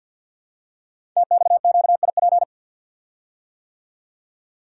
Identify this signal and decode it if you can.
Morse code, heard as a keyed tone, about 52 wpm, 700 Hz. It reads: TXQIP
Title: Code Groups - 5 Letters - 50wpm